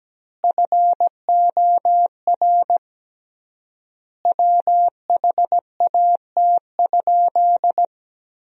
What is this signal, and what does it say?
Morse code, heard as a keyed tone, about 17 wpm, 700 Hz. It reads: FOR WHAT?
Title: 3x Word-Spacing - Sentences from Top 100 Words - 17wpm